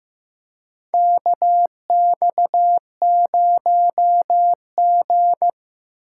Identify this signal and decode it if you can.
Morse code, heard as a keyed tone, about 16 wpm, 700 Hz. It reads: KX0G
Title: USA Call Signs - 4 Characters - 4x Character Spacing - 15wpm